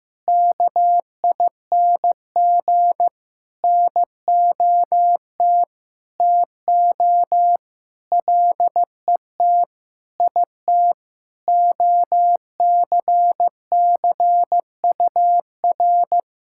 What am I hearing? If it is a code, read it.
Morse code, heard as a keyed tone, about 16 wpm, 700 Hz. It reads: KING NOT TO LET IT OCCUR
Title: Sentences from Top 1000 Words - Review Part 6 - 15wpm